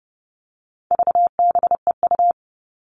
Morse code, heard as a keyed tone, about 30 wpm, 700 Hz. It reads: VBEU
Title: All Cut Numbers - 4-Digit 30wpm